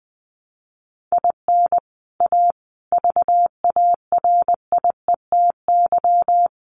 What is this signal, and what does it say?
Morse code, heard as a keyed tone, about 20 wpm, 700 Hz. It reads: IN A VARIETY
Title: Sets of 5 Words Encore 20wpm